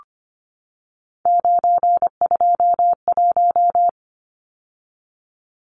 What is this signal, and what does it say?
Morse code, heard as a keyed tone, about 25 wpm, 700 Hz. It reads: 921
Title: Numbers - 3-Digit 25wpm